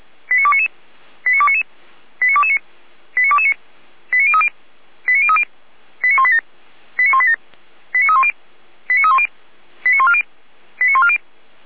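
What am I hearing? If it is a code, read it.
Five-tone selective call (ZVEI), heard as quick runs of five tones, tones of about 80 ms. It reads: 89299 89299 89229 89229 89030 89030 89187 89187 89310 89310 89140 89140
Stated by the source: Five-tone ZVEI selective call received by SDR.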